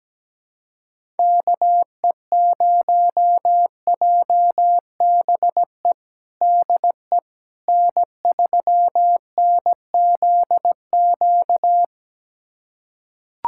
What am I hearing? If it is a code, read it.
Morse code, heard as a keyed tone, about 17 wpm, 700 Hz. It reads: KE0JBE DE N3NZQ
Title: USA Callsign DE Callsign 17wpm